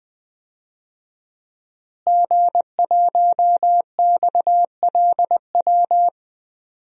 Morse code, heard as a keyed tone, about 20 wpm, 700 Hz. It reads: G1XLW
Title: DXCC-100 Call Signs - 5 Characters - Repeated 2x - 20wpm